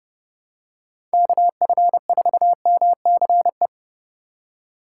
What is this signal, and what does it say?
Morse code, heard as a keyed tone, about 30 wpm, 700 Hz. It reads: KF4MCE